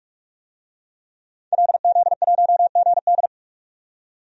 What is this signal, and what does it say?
Morse code, heard as a keyed tone, about 46 wpm, 700 Hz. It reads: LZ1GD
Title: DXCC-100 Call Signs - 5 Characters - 45wpm